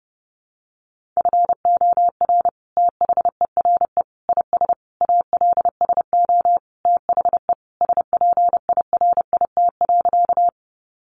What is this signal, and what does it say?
Morse code, heard as a keyed tone, about 30 wpm, 700 Hz. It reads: FOR THERE IS ALSO THE SPIRIT.
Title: Sentences from Top 1400 Words 30wpm